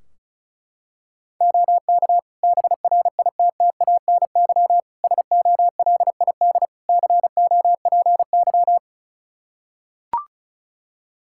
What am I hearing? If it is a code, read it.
Morse code, heard as a keyed tone, about 36 wpm, 700 Hz. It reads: OK BRITTANY SOLID COPY
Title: K1USN SST QSO Practice 35wpm